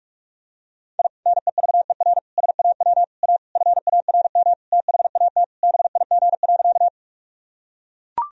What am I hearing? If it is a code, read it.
Morse code, heard as a keyed tone, about 43 wpm, 700 Hz. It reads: I NEVER SAW A FARM THAT BIG.
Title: Sentences from Top 1000 Words - Review Part 2 - 45wpm